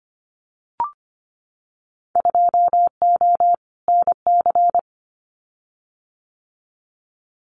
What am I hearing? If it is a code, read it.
Morse code, heard as a keyed tone, about 25 wpm, 700 Hz. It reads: 2O NC
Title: Winter Field Day Exchanges 25wpm